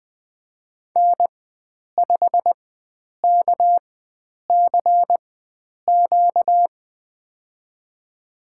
Morse code, heard as a keyed tone, about 20 wpm, spaced as for 9 wpm, 700 Hz. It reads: N5KCQ
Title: USA Call Signs - 5 Characters - 4x Character Spacing - 20wpm